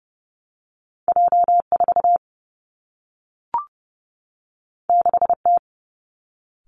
Morse code, heard as a keyed tone, about 30 wpm, 700 Hz. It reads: J 4 6 T
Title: Two Challenging Letter-Number Combinations 30wpm